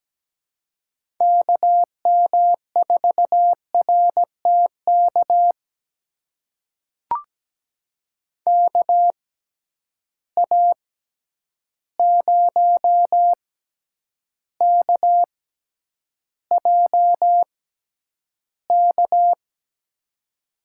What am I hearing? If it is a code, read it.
Morse code, heard as a keyed tone, about 17 wpm, spaced as for 9 wpm, 700 Hz. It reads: KM4RTK KA0KJK
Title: USA Call Signs - 6 Characters - 6x Character Spacing - 17wpm